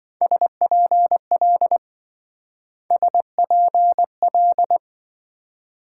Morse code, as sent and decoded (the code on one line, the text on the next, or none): ... .--. .-.. / ... .--. .-..
SPL SPL